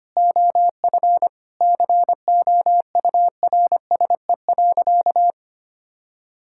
Morse code, heard as a keyed tone, about 25 wpm, 700 Hz. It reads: OF COURSE.
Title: Sentences from Top 1000 Words - Review Part 5 - 25wpm